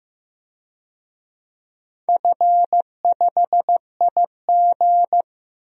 Morse code, heard as a keyed tone, about 15 wpm, 700 Hz. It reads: F5IG
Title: DXCC-100 Call Signs - 4 Characters - Repeated 3x - 15wpm